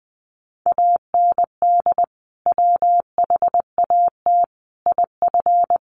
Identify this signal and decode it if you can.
Morse code, heard as a keyed tone, about 19 wpm, 700 Hz. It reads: AND WHAT IF
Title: Sentences from Top 1500 Words 20wpm